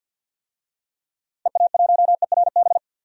Morse code, heard as a keyed tone, about 46 wpm, 700 Hz. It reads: EA1ERB
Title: DXCC-100 Call Signs - 6 Characters - Repeated 3x - 50wpm